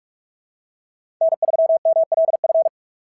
Morse code, heard as a keyed tone, about 44 wpm, 660 Hz. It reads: N3MLF
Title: SOTA QSO Practice 45wpm